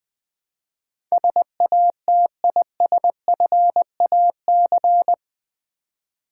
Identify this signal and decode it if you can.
Morse code, heard as a keyed tone, about 20 wpm, 700 Hz. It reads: SATISFAC
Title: Build A Word - 4 Letters At A Time - 12-Letter Words - 20wpm